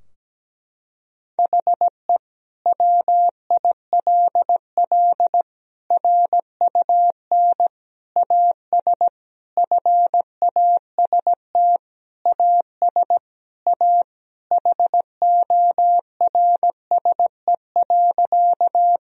Morse code, heard as a keyed tone, about 17 wpm, 700 Hz. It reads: HE WILL RUN AS FAST AS A HORSE.
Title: Sentences from Top 1000 Words - Review Part 2 - 17wpm